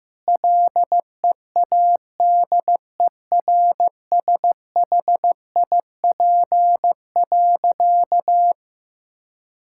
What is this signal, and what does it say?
Morse code, heard as a keyed tone, about 15 wpm, 700 Hz. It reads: LEADERSHIP.
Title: Sentences from Top 1600 Words 15wpm